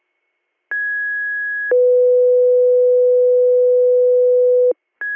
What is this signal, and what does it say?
Two-tone page, heard as a one-second tone, then a three-second tone, about 1650 then 500 Hz. It is heard from 0.7 to 4.7 s.